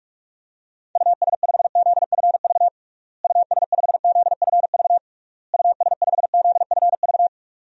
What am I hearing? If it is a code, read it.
Morse code, heard as a keyed tone, about 46 wpm, 700 Hz. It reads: US5ZFV US5ZFV US5ZFV
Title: DXCC-100 Call Signs - 6 Characters - Repeated 3x - 45wpm